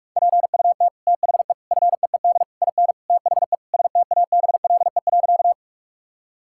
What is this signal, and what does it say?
Morse code, heard as a keyed tone, about 46 wpm, 700 Hz. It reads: PUT THE FEED IN THE STABLE.
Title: Sentences from Top 1600 Words 45wpm